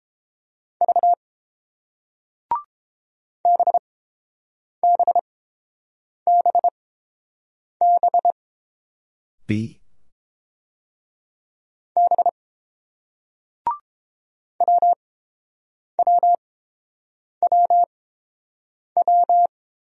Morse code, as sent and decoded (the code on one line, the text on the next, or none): ...- / -... -... -... -... / -... / .-- .-- .-- .--
V BBBB B WWWW